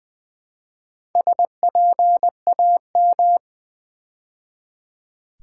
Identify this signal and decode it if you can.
Morse code, heard as a keyed tone, about 20 wpm, 700 Hz. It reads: SPAM
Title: Build A Word - 2 Letters At A Time - 4-Letter Words - 20wpm